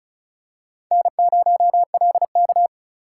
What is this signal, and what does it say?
Morse code, heard as a keyed tone, about 36 wpm, 700 Hz. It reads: N0LK
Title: USA Call Signs - 4 Characters - 2x Character Spacing - 35wpm